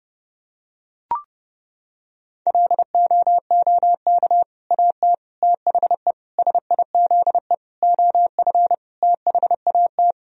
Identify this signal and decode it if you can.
Morse code, heard as a keyed tone, about 30 wpm, 700 Hz. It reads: LOOK AT THE SIZE OF THAT
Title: Sentences from Top 1000 Words - Review Part 3 - 30wpm